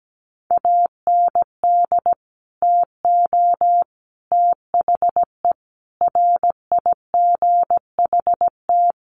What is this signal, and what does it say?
Morse code, heard as a keyed tone, about 17 wpm, 700 Hz. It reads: AND TO THE RIGHT
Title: Sentences from Top 1300 Words 17wpm